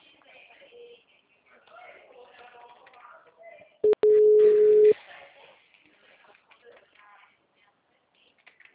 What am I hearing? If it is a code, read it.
Ringback tone (Europe, 425 Hz), with speech.